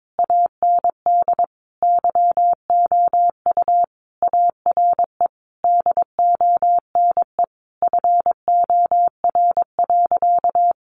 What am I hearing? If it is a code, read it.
Morse code, heard as a keyed tone, about 22 wpm, 700 Hz. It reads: AND YOU ARE DONE FOR.
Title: Sentences from Top 1500 Words 22wpm